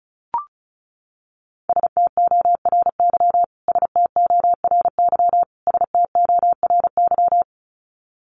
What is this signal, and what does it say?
Morse code, heard as a keyed tone, about 34 wpm, 700 Hz. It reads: STORY STORY STORY
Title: Top 500 Words Repeated 35wpm